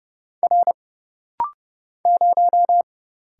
Morse code, heard as a keyed tone, about 30 wpm, 700 Hz. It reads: R 0
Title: Single Letter-Number Alt - Rapid-Fire 30wpm